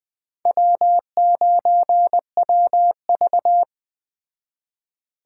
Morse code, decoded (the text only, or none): W9WV